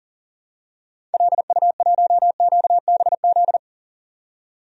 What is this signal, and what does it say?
Morse code, heard as a keyed tone, about 40 wpm, 700 Hz. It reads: LU1QBZ
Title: DXCC-100 Call Signs - 6 Characters - 40wpm